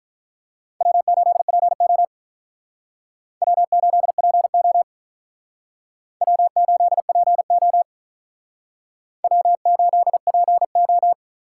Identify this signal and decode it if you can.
Morse code, heard as a keyed tone, about 43 wpm, 700 Hz. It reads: W8PO W8PO W8PO W8PO
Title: Speed Racing - USA 1x2 Call Signs 35wpm